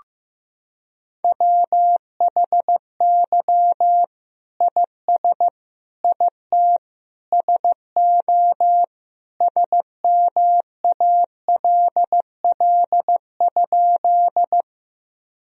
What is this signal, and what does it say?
Morse code, heard as a keyed tone, about 15 wpm, 700 Hz. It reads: WHY IS IT SO SMALL?
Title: Sentences from Top 1000 Words - Review Part 1 - 15wpm